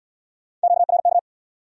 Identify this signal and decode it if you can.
Morse code, heard as a keyed tone, about 75 wpm, 700 Hz. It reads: QRK